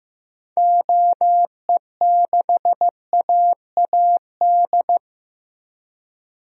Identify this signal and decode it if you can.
Morse code, heard as a keyed tone, about 15 wpm, 700 Hz. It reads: OE6AAD
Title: DXCC-100 Call Signs - 6 Characters - Repeated 3x - 15wpm